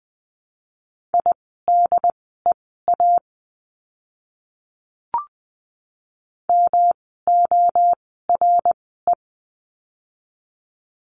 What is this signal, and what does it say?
Morse code, heard as a keyed tone, about 20 wpm, spaced as for 14 wpm, 700 Hz. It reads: IDEA MORE
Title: 108 Lesson 18f - Letter m - Words 20wpm